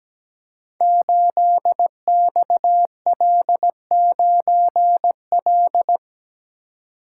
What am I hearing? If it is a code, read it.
Morse code, heard as a keyed tone, about 17 wpm, 700 Hz. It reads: 8XL9L
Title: Code Groups - 5 Letter:Numbers - 17wpm